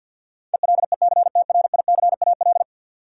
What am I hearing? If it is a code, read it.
Morse code, heard as a keyed tone, about 52 wpm, 700 Hz. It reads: ELECTRICAL
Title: Sentences from Top 1000 Words - Review Part 6 - New Word Drill - 50wpm